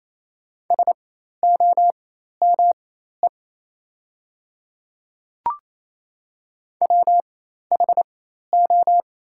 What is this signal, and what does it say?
Morse code, heard as a keyed tone, about 28 wpm, spaced as for 13 wpm, 690 Hz. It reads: SOME WHO
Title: 4x Character Spacing - Top 200 Words - 28wpm